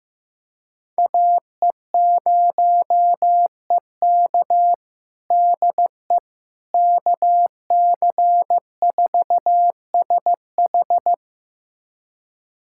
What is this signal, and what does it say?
Morse code, heard as a keyed tone, about 15 wpm, 700 Hz. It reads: AE0EK DE KC4SH
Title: USA Callsign DE Callsign - 5 Characters - 15wpm